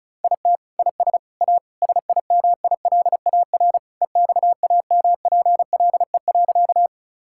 Morse code, heard as a keyed tone, about 36 wpm, 700 Hz. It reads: IT IS A SIMILAR EXAMPLE.
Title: Sentences from Top 1000 Words - Review Part 5 - 35wpm